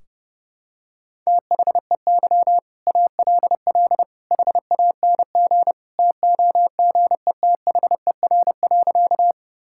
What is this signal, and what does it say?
Morse code, heard as a keyed tone, about 31 wpm, 700 Hz. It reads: THEY ALL HANG TOGETHER.
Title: Sentences from Top 1600 Words 30wpm